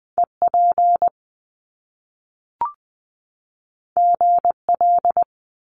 Morse code, heard as a keyed tone, about 20 wpm, 700 Hz. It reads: EP GL
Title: Top Two Letter Combinations in Words 20wpm